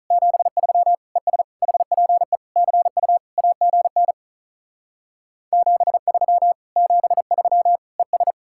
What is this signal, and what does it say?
Morse code, heard as a keyed tone, about 43 wpm, 700 Hz. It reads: 73 ES HPE CU AGN 73 73 ES
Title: Speed-Racing Common QSO Phrases Encore - 35wpm